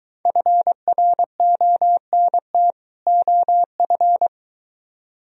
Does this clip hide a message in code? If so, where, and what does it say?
Yes: Morse code from 0.3 to 4.3 s, heard as a keyed tone, about 23 wpm, 700 Hz. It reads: FRONT OF